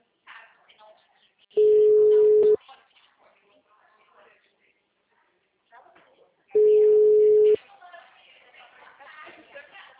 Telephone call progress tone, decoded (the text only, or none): ringback tone